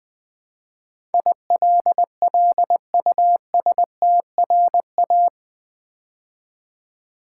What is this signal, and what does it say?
Morse code, heard as a keyed tone, about 20 wpm, 700 Hz. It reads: ILLUSTRA
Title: Build A Word - 4 Letters At A Time - 12-Letter Words - 20wpm